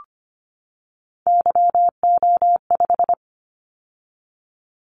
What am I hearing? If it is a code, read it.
Morse code, heard as a keyed tone, about 25 wpm, 700 Hz. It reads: YO5
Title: Callsign Prefixes for Top 25 DXCC - 25wpm